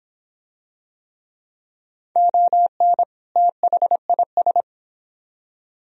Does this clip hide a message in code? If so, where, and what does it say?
Yes: Morse code from 2.2 to 4.6 s, heard as a keyed tone, about 26 wpm, 700 Hz. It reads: ON THIS